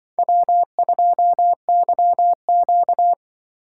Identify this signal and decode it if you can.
Morse code, heard as a keyed tone, about 25 wpm, 700 Hz. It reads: W2YQ